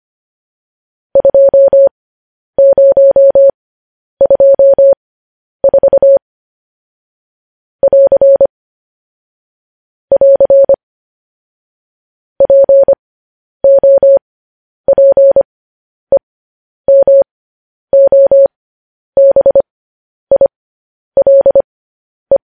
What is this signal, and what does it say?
Morse code, heard as a keyed tone, about 25 wpm, spaced as for 10 wpm, 550 Hz. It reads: 2024 <AR> <AR> POPEMOBILE
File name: podcast / News headlines-1x-25x10